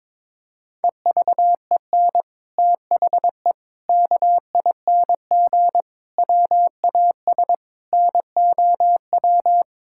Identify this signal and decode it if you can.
Morse code, heard as a keyed tone, about 22 wpm, 700 Hz. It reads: EVEN THE KING WAS NOW
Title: Sentences from Top 1000 Words - Review Part 6 - 22wpm